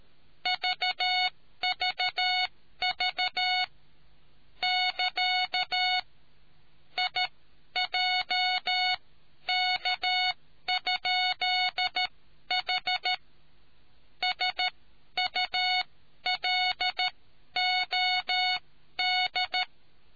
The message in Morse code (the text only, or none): VVV <KA> IJK?H SULOD